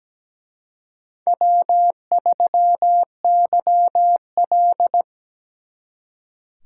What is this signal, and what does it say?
Morse code, heard as a keyed tone, about 20 wpm, 700 Hz. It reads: W3YL
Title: Speed Racing - USA 1x2 Call Signs 17wpm